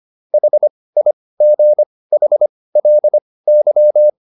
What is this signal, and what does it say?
Morse code, heard as a keyed tone, about 25 wpm, spaced as for 18 wpm, 600 Hz. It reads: HIGHLY
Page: intermediate-practice-files